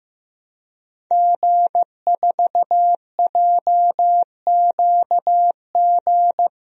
Morse code, heard as a keyed tone, about 15 wpm, 700 Hz. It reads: G4JQG
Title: DXCC-100 Call Signs - 5 Characters - 15wpm